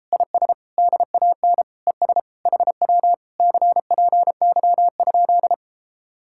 Morse code, heard as a keyed tone, about 33 wpm, 700 Hz. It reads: IS DAN ES HW CPY?